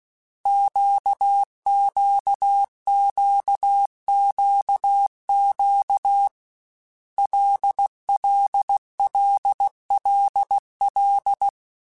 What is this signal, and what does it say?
Morse code, heard as a keyed tone, about 16 wpm, 800 Hz. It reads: QQQQQ LLLLL
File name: Basic Letter Groups By7Groups / LFCYQ / LFCYQ@13